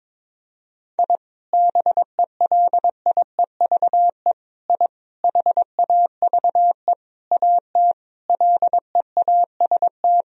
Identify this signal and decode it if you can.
Morse code, heard as a keyed tone, about 21 wpm, 700 Hz. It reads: I BELIEVE I HAVE AT LEAST
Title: Sentences from Top 1000 Words - Review Part 2 - 22wpm